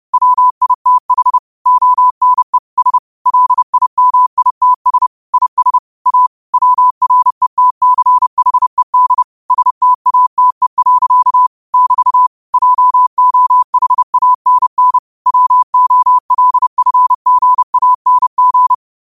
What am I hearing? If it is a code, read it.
Morse code, heard as a keyed tone, about 30 wpm, 1000 Hz. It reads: WITH ONE S LIMITS IS A WRETCHED STATE. <BT> JOHANN WOLFGANG
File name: qotdfiles / 30wpm